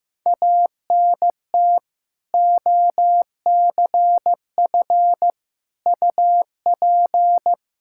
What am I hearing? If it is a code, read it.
Morse code, heard as a keyed tone, about 15 wpm, 700 Hz. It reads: ANT OCF UP
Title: Practice QSO - Exchange 2 - Info - 15wpm